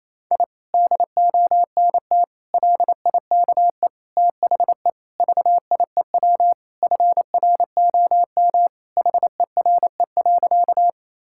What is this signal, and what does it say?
Morse code, heard as a keyed tone, about 27 wpm, 700 Hz. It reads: I DONT LIKE THE VIEW FROM HERE.
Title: Sentences from Top 1000 Words - Review Part 6 - 28wpm